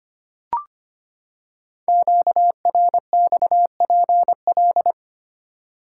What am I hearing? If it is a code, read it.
Morse code, heard as a keyed tone, about 25 wpm, 700 Hz. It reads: QRXPL
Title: Code Groups - 5 Letters - 25wpm